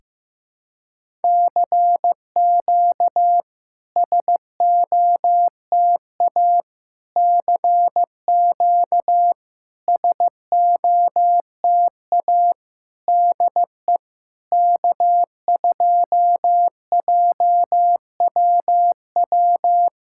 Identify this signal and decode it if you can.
Morse code, heard as a keyed tone, about 15 wpm, 700 Hz. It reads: CQ SOTA CQ SOTA DE K2JWW
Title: SOTA QSO Practice 15wpm